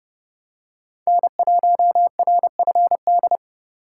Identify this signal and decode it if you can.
Morse code, heard as a keyed tone, about 28 wpm, 700 Hz. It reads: N1RFD